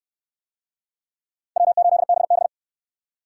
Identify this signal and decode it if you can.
Morse code, heard as a keyed tone, about 65 wpm, 700 Hz. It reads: W8BB